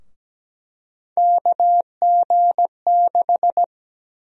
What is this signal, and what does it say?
Morse code, heard as a keyed tone, about 17 wpm, 700 Hz. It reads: KG6
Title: USA Call Sign Prefixes - 17wpm